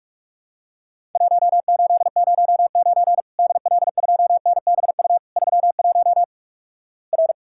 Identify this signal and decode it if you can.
Morse code, heard as a keyed tone, about 45 wpm, 700 Hz. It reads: 1809 DL2NBU 31 R
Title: Worked All Europe DX Contest - QTC Exchange - 45wpm